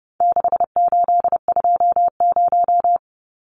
Morse code, heard as a keyed tone, about 30 wpm, 700 Hz. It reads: 6820
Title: Numbers - 4-Digit 30wpm